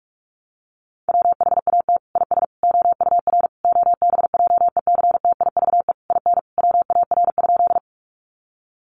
Morse code, heard as a keyed tone, about 44 wpm, 700 Hz. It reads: WHAT IS OUR OBJECTIVE IN WAR?
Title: Sentences from Top 1200 Words 45wpm